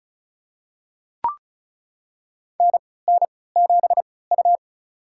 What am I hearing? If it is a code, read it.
Morse code, heard as a keyed tone, about 36 wpm, spaced as for 20 wpm, 700 Hz. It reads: NN7U
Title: USA Call Signs - 4 Characters - 3x Character Spacing - 35wpm